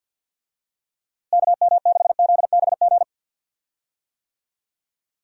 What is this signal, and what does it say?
Morse code, heard as a keyed tone, about 50 wpm, 700 Hz. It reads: KM6ZBG
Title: USA Call Signs - 6 Characters - Repeated 2x - 50wpm